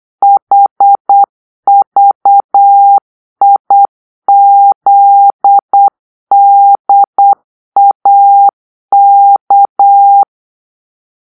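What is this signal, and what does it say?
Morse code, heard as a keyed tone, about 8 wpm, 800 Hz. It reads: HVIZDAK
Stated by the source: Morse-code cipher made by a player team for the Kachna puzzle game.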